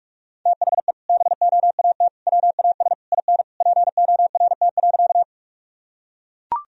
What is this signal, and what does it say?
Morse code, heard as a keyed tone, about 44 wpm, 700 Hz. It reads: THE BOAT WAS IN PORT.
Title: Sentences from Top 1000 Words - Review Part 1 - 45wpm